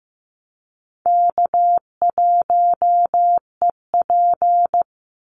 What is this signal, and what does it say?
Morse code, heard as a keyed tone, about 15 wpm, 700 Hz. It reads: K1EP
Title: Call Signs Repeated 3x - 15wpm